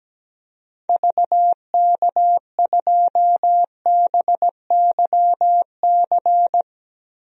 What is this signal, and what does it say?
Morse code, heard as a keyed tone, about 17 wpm, 700 Hz. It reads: VK2BYC
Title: DXCC-100 Call Signs - 6 Characters - Repeated 3x - 17wpm